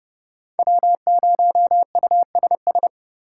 Morse code, heard as a keyed tone, about 26 wpm, 700 Hz. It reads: W0USS